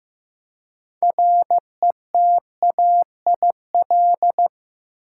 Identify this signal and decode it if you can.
Morse code, heard as a keyed tone, about 15 wpm, 700 Hz. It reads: RETAIL
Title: Practice QSO - Exchange 3 - More Info - 15wpm